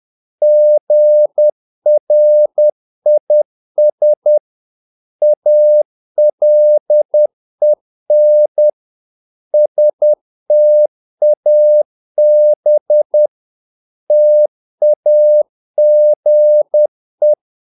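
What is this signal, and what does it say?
Morse code, heard as a keyed tone, about 10 wpm, 600 Hz. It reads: GRIS ALEN STAB TAGE
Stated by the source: Morse practice of short Danish words at 10 wpm, from CW med Gnister.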